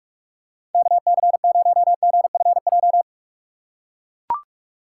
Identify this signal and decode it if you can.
Morse code, heard as a keyed tone, about 42 wpm, 700 Hz. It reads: KC0GFJ
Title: USA Call Signs - 6 Characters - 6x Character Spacing - 45wpm